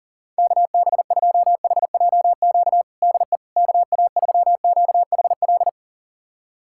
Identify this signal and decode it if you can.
Morse code, heard as a keyed tone, about 40 wpm, 700 Hz. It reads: KB2HJQ DE KA3QHL
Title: USA Callsign DE Callsign 40wpm